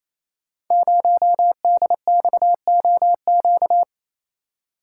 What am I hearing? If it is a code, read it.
Morse code, heard as a keyed tone, about 28 wpm, 690 Hz. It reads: 0DXOQ
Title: Code Groups - 5 Letter:Numbers - 28wpm